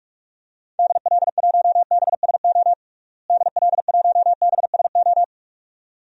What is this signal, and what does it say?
Morse code, heard as a keyed tone, about 46 wpm, 700 Hz. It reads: DL1BSO DL1BSO
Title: DXCC-100 Call Signs - 6 Characters - Repeated 2x - 45wpm